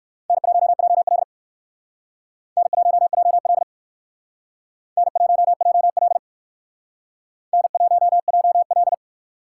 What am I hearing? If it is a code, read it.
Morse code, heard as a keyed tone, about 56 wpm, 700 Hz. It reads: N1JL N1JL N1JL N1JL